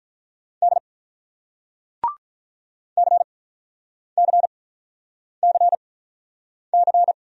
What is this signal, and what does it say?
Morse code, heard as a keyed tone, about 48 wpm, spaced as for 9 wpm, 700 Hz. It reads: D CCCC